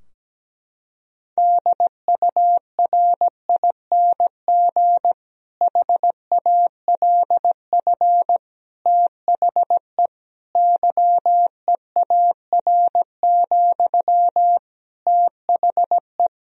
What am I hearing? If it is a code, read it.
Morse code, heard as a keyed tone, about 17 wpm, 700 Hz. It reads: DURING HALF THE YEAR, THE
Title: Sentences from Top 1000 Words - Review Part 2 - 17wpm